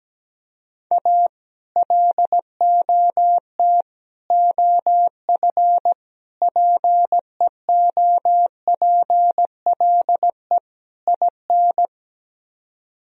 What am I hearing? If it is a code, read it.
Morse code, heard as a keyed tone, about 17 wpm, 700 Hz. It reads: A LOT OF PEOPLE IN
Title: Speed-Racing - Sets of 5 Words 17wpm